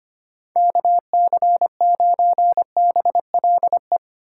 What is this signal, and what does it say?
Morse code, heard as a keyed tone, about 25 wpm, 700 Hz. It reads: KC9BLE
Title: USA Call Signs - 6 Characters - 4x Character Spacing - 25wpm